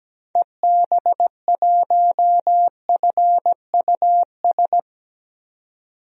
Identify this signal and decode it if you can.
Morse code, heard as a keyed tone, about 17 wpm, 700 Hz. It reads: EB1FUS
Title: DXCC-100 Call Signs - 6 Characters - 17wpm